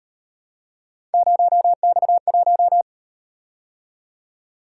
Morse code, heard as a keyed tone, about 38 wpm, 700 Hz. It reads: 0X1